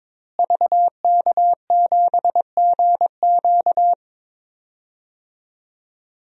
Morse code, heard as a keyed tone, about 22 wpm, 700 Hz. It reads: VK7GQ
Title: DXCC-100 Call Signs - 5 Characters - Repeated 2x - 22wpm